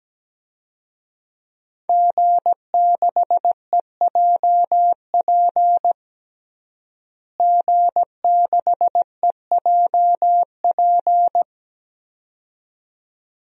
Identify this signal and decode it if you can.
Morse code, heard as a keyed tone, about 17 wpm, 700 Hz. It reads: G6EJP G6EJP
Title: DXCC-100 Call Signs - 5 Characters - Repeated 2x - 17wpm